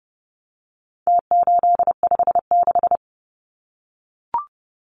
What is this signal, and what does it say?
Morse code, heard as a keyed tone, about 30 wpm, 700 Hz. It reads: T856
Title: Numbers - Contest Style - 4-Digit 30wpm